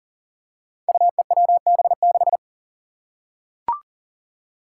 Morse code, heard as a keyed tone, about 40 wpm, 700 Hz. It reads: UEWB6
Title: Five Challenging Letter-Number Combinations 40wpm